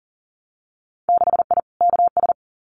Morse code, heard as a keyed tone, about 40 wpm, 700 Hz. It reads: 6I KS
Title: Winter Field Day Exchanges Encore 40wpm